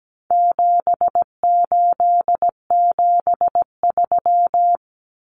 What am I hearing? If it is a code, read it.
Morse code, heard as a keyed tone, about 17 wpm, 700 Hz. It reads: 7873
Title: Common Cut Numbers - 4-Digit 17wpm